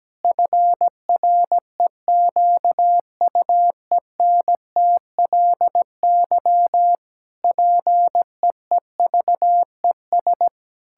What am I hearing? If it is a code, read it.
Morse code, heard as a keyed tone, about 17 wpm, 700 Hz. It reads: FREQUENTLY PEEVES
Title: Pangram Challenge v2 - 17wpm